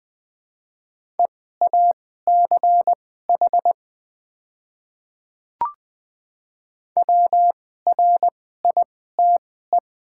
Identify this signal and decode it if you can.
Morse code, heard as a keyed tone, about 20 wpm, spaced as for 14 wpm, 700 Hz. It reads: EACH WRITE
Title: 2x Character Spacing - Top 100 Words - 20wpm